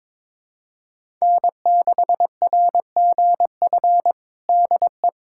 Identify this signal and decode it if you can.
Morse code, heard as a keyed tone, about 22 wpm, 700 Hz. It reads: N6RGF DE
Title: USA Callsign DE Callsign - 5 Characters - 22wpm